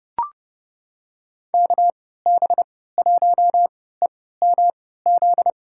Morse code, heard as a keyed tone, about 30 wpm, spaced as for 17 wpm, 700 Hz. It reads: KB1EMZ
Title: USA Call Signs - 6 Characters - 3x Character Spacing - 30wpm